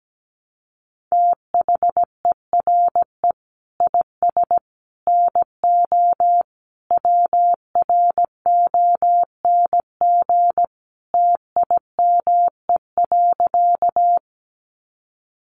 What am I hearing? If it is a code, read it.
Morse code, heard as a keyed tone, about 17 wpm, 700 Hz. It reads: THERE IS NO WRONG TIME.
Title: Sentences from Top 900 Words 17wpm